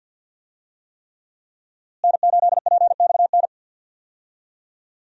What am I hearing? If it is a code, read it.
Morse code, heard as a keyed tone, about 46 wpm, 700 Hz. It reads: N8PXN
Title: USA Call Signs - 5 Characters - Repeated 3x - 50wpm